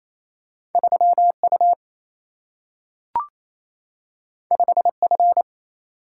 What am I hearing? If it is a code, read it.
Morse code, heard as a keyed tone, about 28 wpm, 700 Hz. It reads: 3U 5F
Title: Single-Letter Single-Number 28wpm